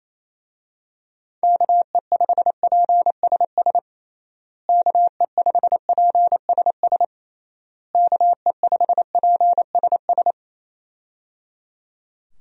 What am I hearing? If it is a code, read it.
Morse code, heard as a keyed tone, about 28 wpm, 700 Hz. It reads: KE5PSS KE5PSS KE5PSS
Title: USA Call Signs - 6 Characters - Repeated 3x - 28wpm